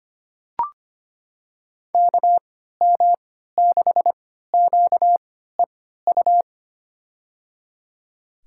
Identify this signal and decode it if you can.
Morse code, heard as a keyed tone, about 25 wpm, spaced as for 14 wpm, 700 Hz. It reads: KM6QEU
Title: USA Call Signs - 6 Characters - 3x Character Spacing - 25wpm